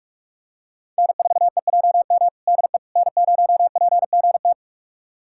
Morse code, heard as a keyed tone, about 43 wpm, 700 Hz. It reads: N4EJM DE N0PGT
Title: USA Callsign DE Callsign - 5 Characters - 45wpm